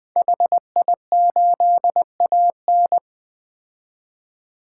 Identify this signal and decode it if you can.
Morse code, heard as a keyed tone, about 20 wpm, 700 Hz. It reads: HI8AN
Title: 2023 POTA Activator Call Signs - 20wpm